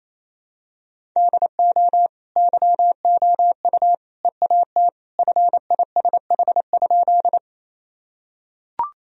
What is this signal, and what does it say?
Morse code, heard as a keyed tone, about 28 wpm, 700 Hz. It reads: DO YOU EAT FISH?
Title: Sentences from Top 1000 Words - Review Part 1 - 28wpm